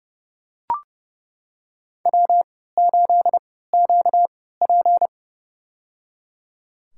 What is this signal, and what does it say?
Morse code, heard as a keyed tone, about 29 wpm, spaced as for 17 wpm, 700 Hz. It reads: W8QP
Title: USA Call Signs - 4 Characters - 3x Character Spacing - 30wpm